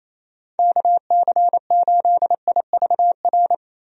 Morse code, heard as a keyed tone, about 28 wpm, 700 Hz. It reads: KC8IVR
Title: USA Call Signs - 6 Characters - Repeated 2x - 28wpm